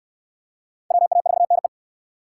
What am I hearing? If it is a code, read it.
Morse code, heard as a keyed tone, about 60 wpm, 700 Hz. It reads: WA4NE